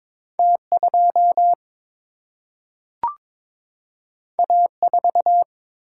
Morse code, heard as a keyed tone, about 22 wpm, 700 Hz. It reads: T 2 A 4
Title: Single-Letter Single-Number 22wpm